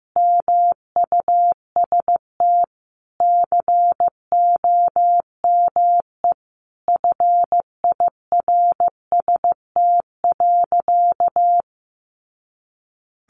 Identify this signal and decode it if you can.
Morse code, heard as a keyed tone, about 15 wpm, 700 Hz. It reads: MUST COME FIRST.
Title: Sentences from Top 300 Words 15wpm